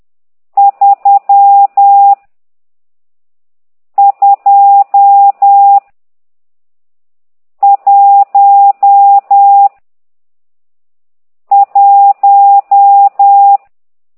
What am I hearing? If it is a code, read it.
Morse code, heard as a keyed tone, about 10 wpm, 800 Hz. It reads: 3 2 1 1
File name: ct2idw / exerc12